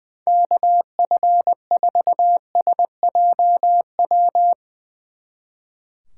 Morse code, heard as a keyed tone, about 22 wpm, 700 Hz. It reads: KF4SJW
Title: Speed Racing - USA Call Signs 20wpm